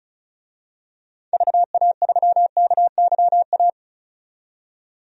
Morse code, heard as a keyed tone, about 34 wpm, 700 Hz. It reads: VA3KYA
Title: DXCC-100 Call Signs - 6 Characters - 35wpm